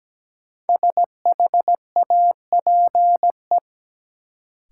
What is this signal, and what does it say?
Morse code, heard as a keyed tone, about 17 wpm, 700 Hz. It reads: SHAPE
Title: Sets of 5 Words Encore - New Word Drill - 17wpm